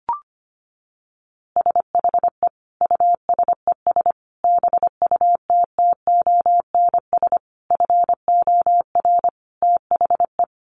Morse code, heard as a keyed tone, about 25 wpm, 700 Hz. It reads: SHE USES BUTTONS FOR THE
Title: Easy Phrases 25wpm